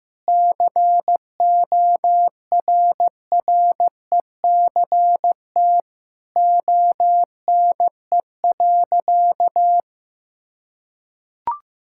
Morse code, heard as a keyed tone, about 14 wpm, 700 Hz. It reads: CORRECT ONE.
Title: Sentences from Top 1000 Words - Review Part 6 - 15wpm